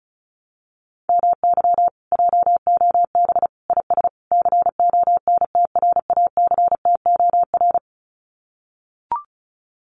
Morse code, heard as a keyed tone, about 36 wpm, 700 Hz. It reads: MY JOB IS CONTRACTOR
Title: Common QSO Phrases Encore 35wpm